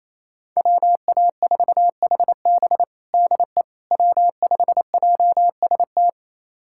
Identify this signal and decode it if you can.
Morse code, heard as a keyed tone, about 27 wpm, 700 Hz. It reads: WA4HB DE W5JST
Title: USA Callsign DE Callsign - 5 Characters - 28wpm